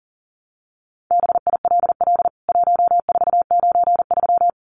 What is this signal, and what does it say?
Morse code, heard as a keyed tone, about 40 wpm, 700 Hz. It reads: BILL 1493
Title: CWT Exchange 40wpm